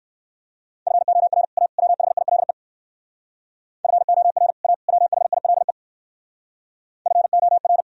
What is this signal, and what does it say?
Morse code, heard as a keyed tone, about 58 wpm, 700 Hz. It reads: FOR A WHILE FOR A WHILE FOR